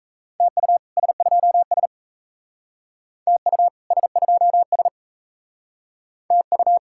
Morse code, heard as a keyed tone, about 36 wpm, 700 Hz. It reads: TU S2S TU S2S TU